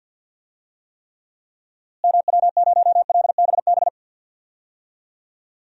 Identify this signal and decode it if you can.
Morse code, heard as a keyed tone, about 47 wpm, 700 Hz. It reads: MW0LBB
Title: DXCC-100 Call Signs - 6 Characters - Repeated 2x - 50wpm